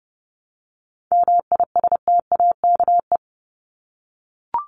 Morse code, heard as a keyed tone, about 30 wpm, 700 Hz. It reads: MISTAKE
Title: Top 1501-1750 Words 30wpm